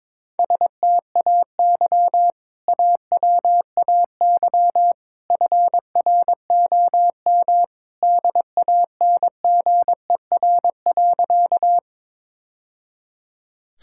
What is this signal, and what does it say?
Morse code, heard as a keyed tone, about 22 wpm, 700 Hz. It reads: STAY AWAY FROM DANGER.
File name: practice / 22wpm / Sentences with 2-Word Phrases - 22wpm